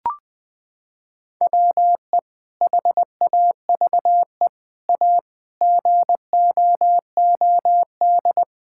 Morse code, heard as a keyed tone, about 21 wpm, 700 Hz. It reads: WE HAVE A GOOD
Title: Sentences from Top 1000 Words - Review Part 5 - 20wpm